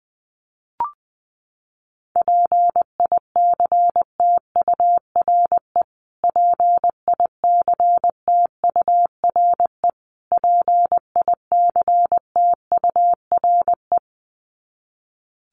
Morse code, heard as a keyed tone, about 20 wpm, 700 Hz. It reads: PICTURE PICTURE PICTURE
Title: Top 500 Words Repeated 20wpm